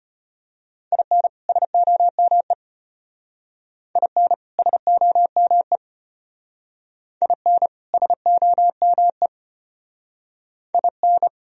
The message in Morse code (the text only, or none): IN SOME IN SOME IN SOME IN